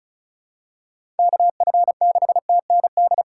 Morse code, heard as a keyed tone, about 36 wpm, 700 Hz. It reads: KF6TND
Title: USA Call Signs - 6 Characters - 2x Character Spacing - 35wpm